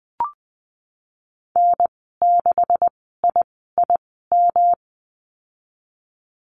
Morse code, heard as a keyed tone, about 20 wpm, spaced as for 15 wpm, 700 Hz. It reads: N6IIM
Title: 154 Lesson 25g - Letter p - Call Signs 20wpm